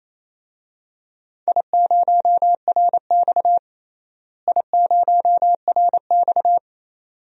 Morse code, heard as a keyed tone, about 28 wpm, 700 Hz. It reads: I0RX I0RX
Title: DXCC-100 Call Signs - 4 Characters - Repeated 2x - 28wpm